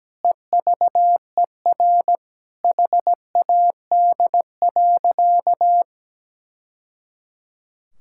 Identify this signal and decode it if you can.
Morse code, heard as a keyed tone, about 17 wpm, 700 Hz. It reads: EVER HAD.
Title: Sentences from Top 1000 Words - Review Part 6 - 17wpm